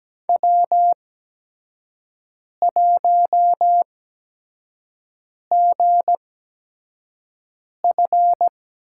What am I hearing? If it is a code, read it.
Morse code, heard as a keyed tone, about 17 wpm, spaced as for 5 wpm, 700 Hz. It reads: W1GF
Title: USA Call Signs - 8x Character Spacing - 17wpm